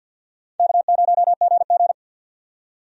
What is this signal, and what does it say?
Morse code, heard as a keyed tone, about 52 wpm, 700 Hz. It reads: K0GG
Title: USA Call Signs - 4 Characters - 50wpm